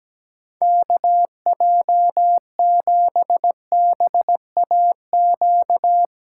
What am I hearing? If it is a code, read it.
Morse code, heard as a keyed tone, about 17 wpm, 700 Hz. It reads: KJ7BAQ
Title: USA Callsign DE Callsign 17wpm